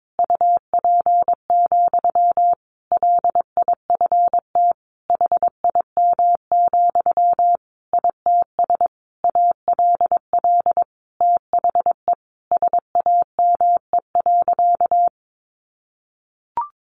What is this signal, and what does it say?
Morse code, heard as a keyed tone, about 22 wpm, 700 Hz. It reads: UP, LIFT HIM, ITS ALL THE SAME.
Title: Sentences from Top 1000 Words 22wpm